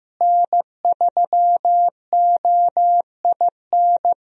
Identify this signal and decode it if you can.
Morse code, heard as a keyed tone, about 15 wpm, 700 Hz. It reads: N3OIN
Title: USA Call Signs - 5 Characters - Repeated 3x - 15wpm